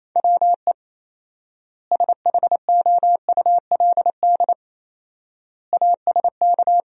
Morse code, heard as a keyed tone, about 28 wpm, 700 Hz. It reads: WE SHOULD ASK
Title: 4x Word-Spacing - Sentences from Top 300 Words - 28wpm